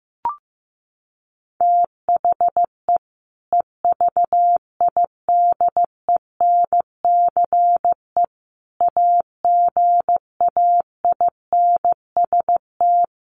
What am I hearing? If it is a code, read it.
Morse code, heard as a keyed tone, about 14 wpm, 700 Hz. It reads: THE EVIDENCE AGAINST
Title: Sentences from Top 1300 Words 15wpm